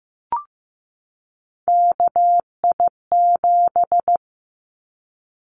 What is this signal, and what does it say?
Morse code, heard as a keyed tone, about 15 wpm, 700 Hz. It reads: KI7
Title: Callsign Prefixes for Top 25 DXCC - 15wpm